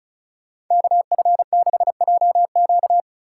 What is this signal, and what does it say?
Morse code, heard as a keyed tone, about 44 wpm, 700 Hz. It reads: KF6JQ
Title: Speed Racing - USA 2x2 Call Signs 35wpm